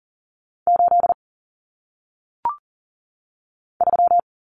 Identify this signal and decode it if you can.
Morse code, heard as a keyed tone, about 39 wpm, 700 Hz. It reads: 8 3
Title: Common Cut Numbers - 1-Digit 40wpm